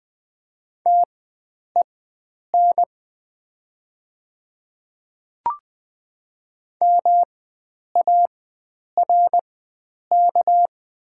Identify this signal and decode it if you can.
Morse code, heard as a keyed tone, about 20 wpm, spaced as for 9 wpm, 700 Hz. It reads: TEN MARK
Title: 4x Character Spacing - Top 500 Words - 20wpm